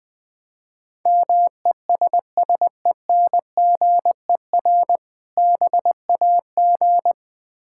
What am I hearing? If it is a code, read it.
Morse code, heard as a keyed tone, about 20 wpm, 700 Hz. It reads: MESSENGER BAG
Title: Difficult Words - Speed-Racing with Hints - 20wpm